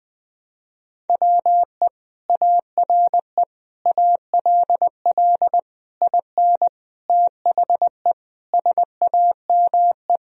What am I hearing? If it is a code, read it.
Morse code, heard as a keyed tone, about 19 wpm, 700 Hz. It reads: WE ARE ALL IN THE SAME
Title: Sentences from Top 1000 Words - Review Part 3 - 20wpm